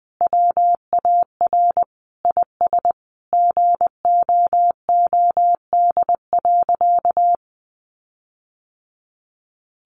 Morse code, heard as a keyed tone, about 20 wpm, 700 Hz. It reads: WAR IS GOOD.
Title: Sentences from Top 1300 Words 20wpm